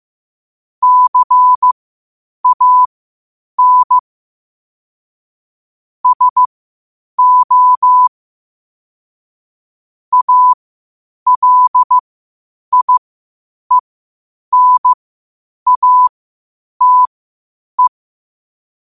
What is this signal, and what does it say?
Morse code, heard as a keyed tone, about 15 wpm, spaced as for 8 wpm, 1000 Hz. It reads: CAN SO ALIENATE
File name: qotdfiles / 8wpm